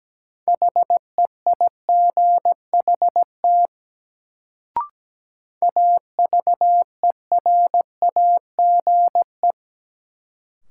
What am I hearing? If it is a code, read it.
Morse code, heard as a keyed tone, about 17 wpm, 700 Hz. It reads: HEIGHT AVERAGE
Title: Sets of 5 Words - New Word Drill - 17wpm